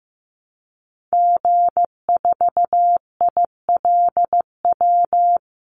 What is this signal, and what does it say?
Morse code, heard as a keyed tone, about 15 wpm, 700 Hz. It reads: G4ILW
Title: Call Signs Repeated 2x - 15wpm